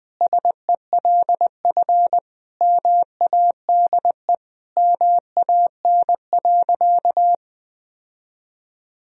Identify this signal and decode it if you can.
Morse code, heard as a keyed tone, about 21 wpm, 700 Hz. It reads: SELF MADE MAN.
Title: Sentences from Top 1000 Words - Review Part 1 - 20wpm